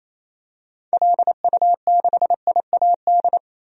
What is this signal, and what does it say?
Morse code, heard as a keyed tone, about 28 wpm, 700 Hz. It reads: LU6IAD
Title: DXCC-100 Call Signs - 6 Characters - 28wpm